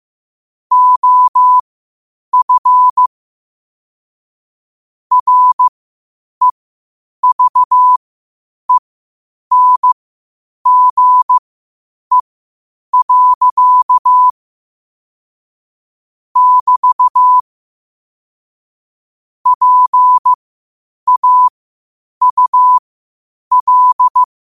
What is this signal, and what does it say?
Morse code, heard as a keyed tone, about 15 wpm, spaced as for 8 wpm, 1000 Hz. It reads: OF REVENGE. <BT> PAUL
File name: qotdfiles / 8wpm